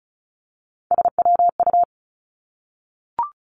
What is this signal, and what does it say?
Morse code, heard as a keyed tone, about 36 wpm, 700 Hz. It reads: SWU
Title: Three Challenging Letter-Number Combinations 35wpm